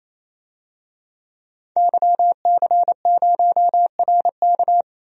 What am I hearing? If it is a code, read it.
Morse code, heard as a keyed tone, about 28 wpm, 700 Hz. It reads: YC0RK
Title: DXCC-100 Call Signs - 5 Characters - 28wpm